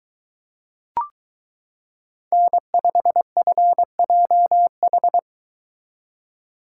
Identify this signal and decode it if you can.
Morse code, heard as a keyed tone, about 22 wpm, 700 Hz. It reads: N5FJH